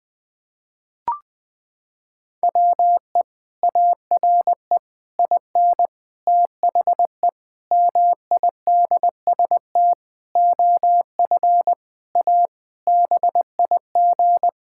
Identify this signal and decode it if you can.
Morse code, heard as a keyed tone, about 21 wpm, 700 Hz. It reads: WE ARE IN THE MIDST OF A BIG
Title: Sentences with 7-Word Phrases - 20wpm